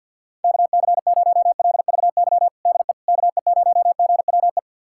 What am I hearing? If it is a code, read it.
Morse code, heard as a keyed tone, about 46 wpm, 700 Hz. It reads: KC0LVY DE KE0GWE
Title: USA Callsign DE Callsign 50wpm